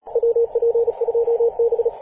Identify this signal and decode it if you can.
Morse code, heard as a keyed tone, about 37 wpm, 470 Hz, with radio noise. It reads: WP2B
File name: posts / 2016-apr-22-2153z-kh8-kc0w-18-080mhz-cw